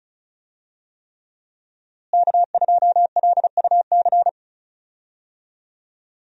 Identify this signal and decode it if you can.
Morse code, heard as a keyed tone, about 36 wpm, 700 Hz. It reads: K2LUC
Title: USA Call Signs - 5 Characters - Repeated 3x - 35wpm